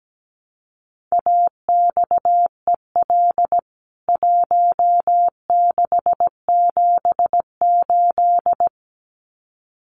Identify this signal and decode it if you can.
Morse code, heard as a keyed tone, about 17 wpm, 700 Hz. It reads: AXEL 1678
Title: CWT Exchange 17wpm